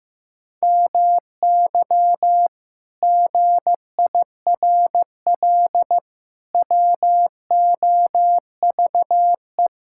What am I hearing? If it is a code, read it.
Morse code, heard as a keyed tone, about 16 wpm, 700 Hz. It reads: MY GIRL WOVE
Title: Pangram Challenge v2 - 15wpm